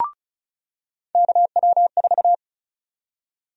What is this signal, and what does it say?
Morse code, heard as a keyed tone, about 34 wpm, 700 Hz. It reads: KW4
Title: USA Call Sign Prefixes - 35wpm